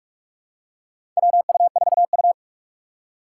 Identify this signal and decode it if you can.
Morse code, heard as a keyed tone, about 46 wpm, 700 Hz. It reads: WU4U
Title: USA Call Signs - 4 Characters - 2x Character Spacing - 45wpm